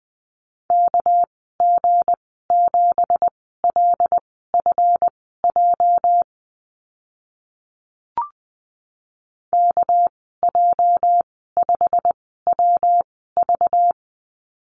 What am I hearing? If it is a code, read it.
Morse code, heard as a keyed tone, about 20 wpm, spaced as for 14 wpm, 700 Hz. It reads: KG7LFJ KJ5WV
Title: 231 Lesson 37g - Letter x - Call Signs 20wpm